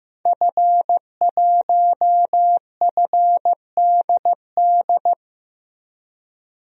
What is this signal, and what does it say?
Morse code, heard as a keyed tone, about 15 wpm, 700 Hz. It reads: F1FDD
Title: DXCC-100 Call Signs - 5 Characters - Repeated 2x - 15wpm